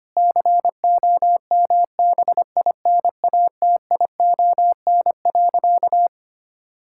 Morse code, heard as a keyed tone, about 25 wpm, 700 Hz. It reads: COMBINATION.
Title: Sentences from Top 1000 Words - Review Part 6 - 25wpm